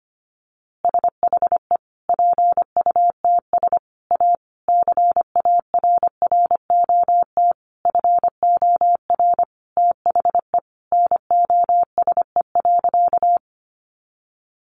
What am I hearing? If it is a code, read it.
Morse code, heard as a keyed tone, about 25 wpm, 700 Hz. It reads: SHE PUTS A CARROT FOR THE NOSE.
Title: Easy Phrases 25wpm